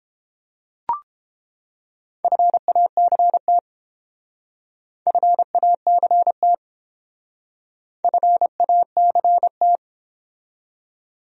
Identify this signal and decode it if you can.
Morse code, heard as a keyed tone, about 28 wpm, 700 Hz. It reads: FACT FACT FACT